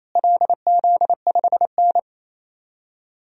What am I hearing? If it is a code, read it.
Morse code, heard as a keyed tone, about 28 wpm, 700 Hz. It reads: LZ5N
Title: DXCC-100 Call Signs - 4 Characters - Repeated 2x - 28wpm